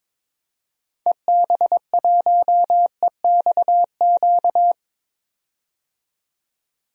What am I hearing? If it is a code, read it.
Morse code, heard as a keyed tone, about 22 wpm, 700 Hz. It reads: EB1EXQ
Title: DXCC-100 Call Signs - 6 Characters - 22wpm